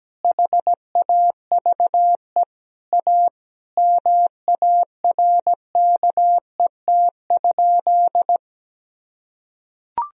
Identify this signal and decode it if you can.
Morse code, heard as a keyed tone, about 17 wpm, 700 Hz. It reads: HAVE A MARKET?
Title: Sentences from Top 1000 Words - Review Part 3 - 17wpm